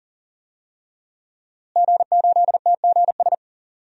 Morse code, heard as a keyed tone, about 40 wpm, 700 Hz. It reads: G8TGS
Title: DXCC-100 Call Signs - 5 Characters - Repeated 2x - 40wpm